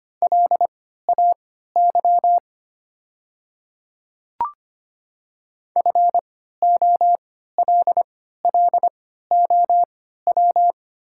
Morse code, heard as a keyed tone, about 25 wpm, spaced as for 14 wpm, 700 Hz. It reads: LAY FOLLOW
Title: 3x Character Spacing - Top 500 Words - 25wpm